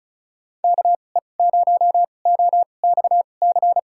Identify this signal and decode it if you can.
Morse code, heard as a keyed tone, about 36 wpm, spaced as for 25 wpm, 700 Hz. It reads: KE0OXC
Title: USA Call Signs - 6 Characters - 2x Character Spacing - 35wpm